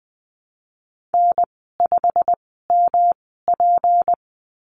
Morse code, heard as a keyed tone, about 20 wpm, spaced as for 14 wpm, 700 Hz. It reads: N5MP
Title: 186 Lesson 30g - Letter b - Call Signs 20wpm